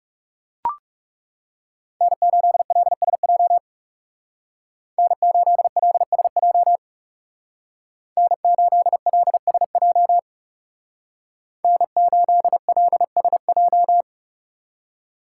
Morse code, heard as a keyed tone, about 39 wpm, 700 Hz. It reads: N8LSJ N8LSJ N8LSJ N8LSJ